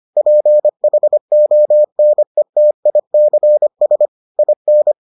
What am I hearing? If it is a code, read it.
Morse code, heard as a keyed tone, about 25 wpm, 600 Hz. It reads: PHONETICS IN